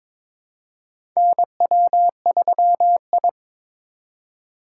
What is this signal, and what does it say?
Morse code, heard as a keyed tone, about 22 wpm, 700 Hz. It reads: NW3I
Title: USA Call Signs - 4 Characters - 4x Character Spacing - 22wpm